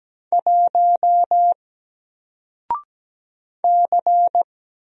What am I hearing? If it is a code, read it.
Morse code, heard as a keyed tone, about 17 wpm, 700 Hz. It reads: 1 C
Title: Single Letter-Number Alt 17wpm